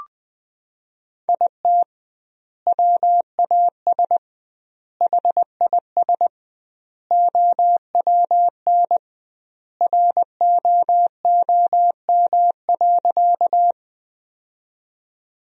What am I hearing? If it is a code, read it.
Morse code, heard as a keyed tone, about 21 wpm, 700 Hz. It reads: IT WAS HIS OWN ROOM.
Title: 2x Word-Spacing - Sentences from Top 300 Words - 20wpm